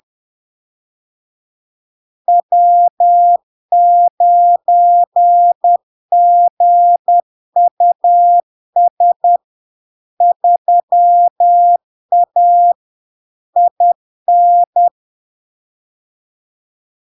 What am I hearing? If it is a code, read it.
Morse code, heard as a keyed tone, about 10 wpm, 700 Hz. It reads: W9GUS 3A IN
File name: LARK-FD-2022-Log-in-CW-10wpm-700-Hz